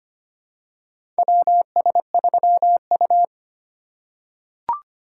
Morse code, heard as a keyed tone, about 25 wpm, 700 Hz. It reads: WS3U
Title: USA Call Signs - 4 Characters - 6x Character Spacing - 25wpm